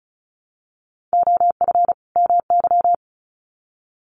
Morse code, heard as a keyed tone, about 34 wpm, 700 Hz. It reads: OF MY
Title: Sets of 2 Words 35wpm